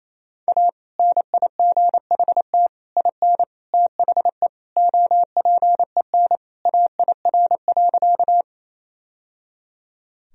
Morse code, heard as a keyed tone, about 28 wpm, 700 Hz. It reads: A NIGHT IN THE OPEN AIR.
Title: Sentences from Top 1000 Words - Review Part 1 - 28wpm